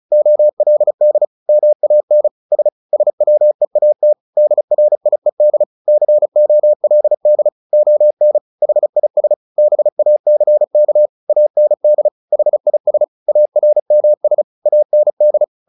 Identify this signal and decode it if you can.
Morse code, heard as a keyed tone, about 34 wpm, 600 Hz. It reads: OLD MAN S SWEAT DRIED COLD ON HIS BACK AND HIS ARMS AND
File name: the old man and the sea / the old man and the sea 1 35wpm